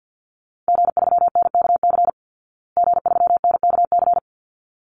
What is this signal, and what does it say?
Morse code, heard as a keyed tone, about 48 wpm, 700 Hz. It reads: G3NKC G3NKC
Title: Call Signs Repeated 3x - 50wpm